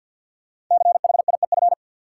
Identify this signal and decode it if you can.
Morse code, heard as a keyed tone, about 47 wpm, 700 Hz. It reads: CHIEF
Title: Sets of 5 Words Encore - New Word Drill - 50wpm